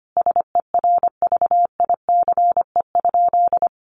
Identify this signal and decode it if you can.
Morse code, heard as a keyed tone, about 25 wpm, 700 Hz. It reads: SERVICE?
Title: Sentences from Top 1500 Words 25wpm